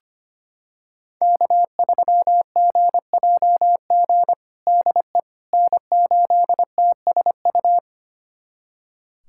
Morse code, heard as a keyed tone, about 25 wpm, 700 Hz. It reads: K3GJG DE N8TSU
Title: USA Callsign DE Callsign - 5 Characters - 25wpm